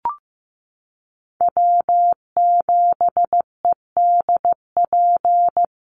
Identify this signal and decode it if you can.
Morse code, heard as a keyed tone, about 15 wpm, 700 Hz. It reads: W7EDP
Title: USA Call Signs 15wpm